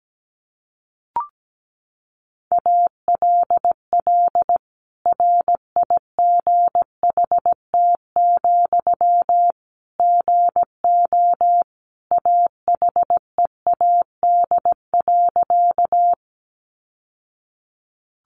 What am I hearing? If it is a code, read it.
Morse code, heard as a keyed tone, about 17 wpm, 700 Hz. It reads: ALL RIGHT, GO AHEAD.
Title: Sentences from Top 1100 Words 17wpm